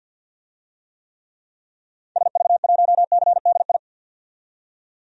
Morse code, heard as a keyed tone, about 48 wpm, 700 Hz. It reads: SV1CDI
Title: DXCC-100 Call Signs - 6 Characters - 50wpm